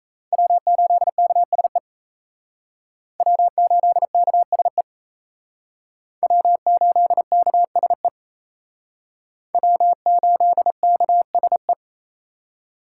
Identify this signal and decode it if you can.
Morse code, heard as a keyed tone, about 35 wpm, 700 Hz. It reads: W8KSE W8KSE W8KSE W8KSE
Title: Speed Racing - USA 1x3 Call Signs 28wpm